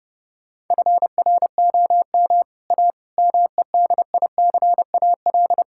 Morse code, heard as a keyed tone, about 30 wpm, 700 Hz. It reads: FROM A MEDICAL
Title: Sentences from Top 1000 Words - Review Part 5 - 30wpm